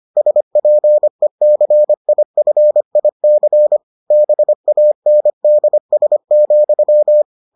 Morse code, heard as a keyed tone, about 25 wpm, 600 Hz. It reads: SPECIFIC BANDS,